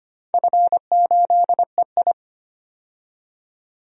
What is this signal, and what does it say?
Morse code, heard as a keyed tone, about 25 wpm, 700 Hz. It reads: F8EI
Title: DXCC-100 Call Signs - 4 Characters - 25wpm